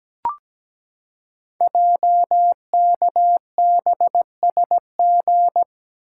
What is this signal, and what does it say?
Morse code, heard as a keyed tone, about 17 wpm, 700 Hz. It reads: JKBSG
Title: Code Groups - 5 Letters - 17wpm